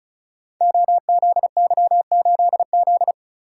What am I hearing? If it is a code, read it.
Morse code, heard as a keyed tone, about 36 wpm, 700 Hz. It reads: OZY8Z
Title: Code Groups - 5 Letter:Numbers - 35wpm